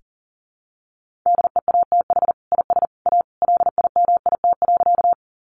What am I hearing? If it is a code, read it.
Morse code, heard as a keyed tone, about 40 wpm, 700 Hz. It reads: DEATH IS A LIMIT.
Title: Sentences from Top 1500 Words 40wpm